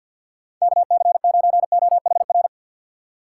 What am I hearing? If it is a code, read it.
Morse code, heard as a keyed tone, about 51 wpm, 700 Hz. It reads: KC9OHR
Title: USA Call Signs - 3x Character Spacing - 50wpm